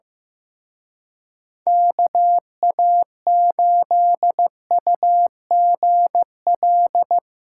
Morse code, heard as a keyed tone, about 17 wpm, 700 Hz. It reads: KA8UGL
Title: Speed Racing - USA Call Signs 15wpm